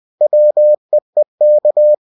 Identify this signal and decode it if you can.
Morse code, heard as a keyed tone, about 20 wpm, 600 Hz. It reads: WEEK
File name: cw-books / winnie-the-pooh / winnie-20wpm-page-0019